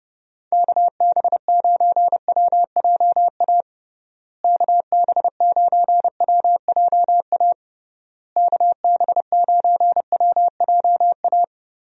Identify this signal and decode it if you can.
Morse code, heard as a keyed tone, about 30 wpm, 700 Hz. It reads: KB9WJA KB9WJA KB9WJA
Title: USA Call Signs - 6 Characters - Repeated 3x - 30wpm